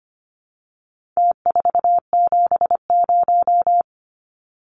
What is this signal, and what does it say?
Morse code, heard as a keyed tone, about 25 wpm, 700 Hz. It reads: T470
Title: Numbers - Contest Style - 4-Digit 25wpm